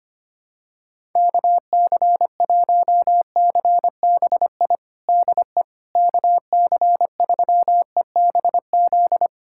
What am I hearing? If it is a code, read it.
Morse code, heard as a keyed tone, about 25 wpm, 700 Hz. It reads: KC1CBI DE KC3EBZ
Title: USA Callsign DE Callsign - 6 Characters - 25wpm